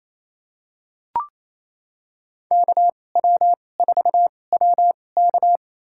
Morse code, heard as a keyed tone, about 28 wpm, spaced as for 20 wpm, 700 Hz. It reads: KW4WK
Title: USA Call Signs - 5 Characters - 2x Character Spacing - 28wpm